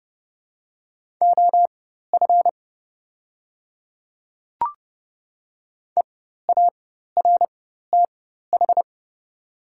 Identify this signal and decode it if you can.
Morse code, heard as a keyed tone, about 30 wpm, spaced as for 14 wpm, 700 Hz. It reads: OF EARTH
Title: 4x Character Spacing - Top 200 Words - 30wpm